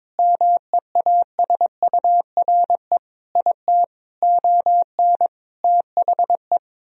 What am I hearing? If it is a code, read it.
Morse code, heard as a keyed tone, about 22 wpm, 700 Hz. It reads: MEASURE IT ON THE
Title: Sentences from Top 1000 Words - Review Part 3 - 22wpm